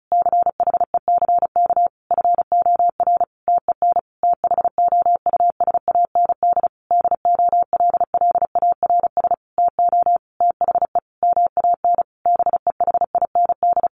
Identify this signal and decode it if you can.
Morse code, heard as a keyed tone, about 34 wpm, 700 Hz. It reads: CHECK FOR TEN THOUSAND DOLLARS TO THE MAN BEHIND
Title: A Sound Of Thunder 35wpm